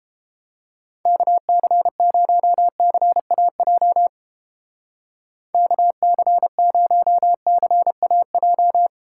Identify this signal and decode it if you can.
Morse code, heard as a keyed tone, about 29 wpm, 700 Hz. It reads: KC0CAJ KC0CAJ